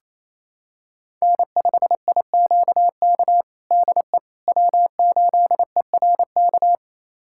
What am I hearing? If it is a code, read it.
Morse code, heard as a keyed tone, about 27 wpm, 700 Hz. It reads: N5IQK DE W8ERK
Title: USA Callsign DE Callsign - 5 Characters - 28wpm